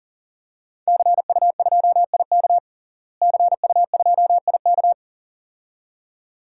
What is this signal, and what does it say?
Morse code, heard as a keyed tone, about 40 wpm, 700 Hz. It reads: CU2IK CU2IK
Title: DXCC-100 Call Signs - 5 Characters - Repeated 2x - 40wpm